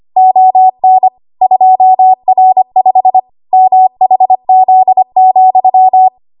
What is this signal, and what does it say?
Morse code, heard as a keyed tone, about 25 wpm, 750 Hz. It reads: ON 2R5 MHZ,
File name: Archive / 25 WPM / 25WPM